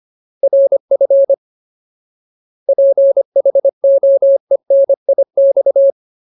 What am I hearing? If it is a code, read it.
Morse code, heard as a keyed tone, about 25 wpm, 550 Hz. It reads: RF PHOENIX
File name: Ultimate Word List / 25WPM Ultimate